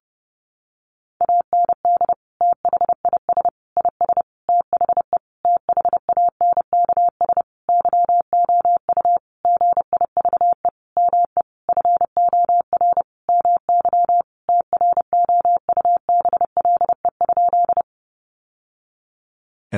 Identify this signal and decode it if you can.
Morse code, heard as a keyed tone, about 28 wpm, 700 Hz. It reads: AND THIS IS THE THANKS YOU GIVE ME FOR MY TROUBLE?
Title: Sentences from Top 1200 Words 30wpm